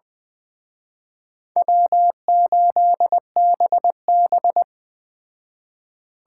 Morse code, heard as a keyed tone, about 20 wpm, 700 Hz. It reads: W8BB
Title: Speed Racing - USA Call Signs 17wpm